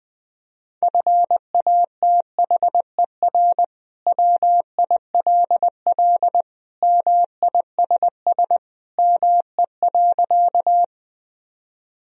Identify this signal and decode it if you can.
Morse code, heard as a keyed tone, about 20 wpm, 700 Hz. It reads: FATHER WILL MISS ME.
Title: Sentences from Top 1000 Words - Review Part 3 - 20wpm